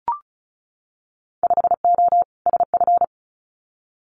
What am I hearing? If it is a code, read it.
Morse code, heard as a keyed tone, about 36 wpm, 700 Hz. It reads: 5O SF
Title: Winter Field Day Exchanges Encore 35wpm